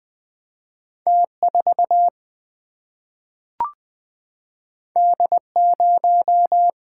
Morse code, heard as a keyed tone, about 20 wpm, 700 Hz. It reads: T4 D0
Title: Single-Letter Single-Number 20wpm